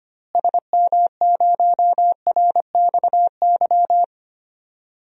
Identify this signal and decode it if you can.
Morse code, heard as a keyed tone, about 25 wpm, 700 Hz. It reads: SM0RXY
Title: DXCC-100 Call Signs - 6 Characters - Repeated 3x - 25wpm